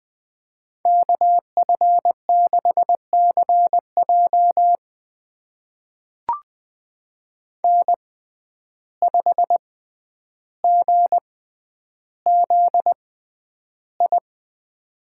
Morse code, heard as a keyed tone, about 21 wpm, spaced as for 10 wpm, 700 Hz. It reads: KF6CJ N5GZI
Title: USA Call Signs - 5 Characters - 6x Character Spacing - 20wpm